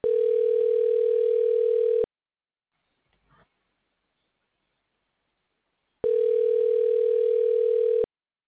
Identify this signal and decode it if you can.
Ringback tone (North America), 440 + 480 Hz.